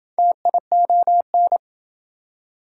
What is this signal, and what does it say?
Morse code, heard as a keyed tone, about 29 wpm, 700 Hz. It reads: TION